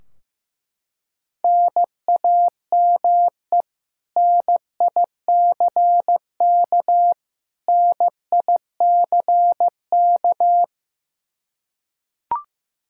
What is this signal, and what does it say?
Morse code, heard as a keyed tone, about 15 wpm, 700 Hz. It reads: NAME NICK NICK
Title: Practice QSO - Exchange 1 - Beginning - 15wpm